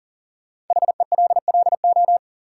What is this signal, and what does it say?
Morse code, heard as a keyed tone, about 40 wpm, 700 Hz. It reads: HELLO
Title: Sentences from Top 1100 Words - New Word Drill - 40wpm